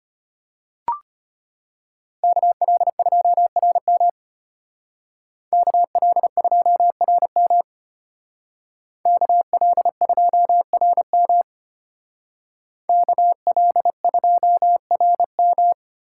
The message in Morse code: KL2RM KL2RM KL2RM KL2RM

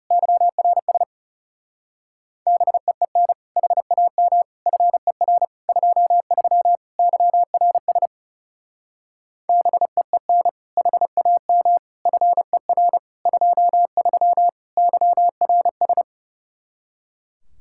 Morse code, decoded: YRS BEEN HAM FER 23 YRS BEEN HAM FER 23 YRS

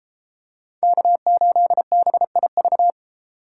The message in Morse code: K8BIV

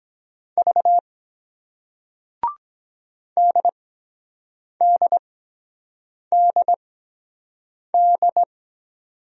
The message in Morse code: V DDDD

